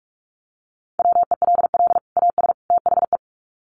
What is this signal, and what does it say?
Morse code, heard as a keyed tone, about 44 wpm, 700 Hz. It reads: WELL AS THE
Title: Sets of 3 Words 45wpm